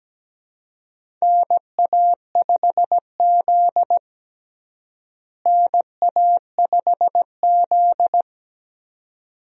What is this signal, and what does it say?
Morse code, heard as a keyed tone, about 17 wpm, 700 Hz. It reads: NA5Z NA5Z
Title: USA Call Signs - 4 Characters - Repeated 2x - 17wpm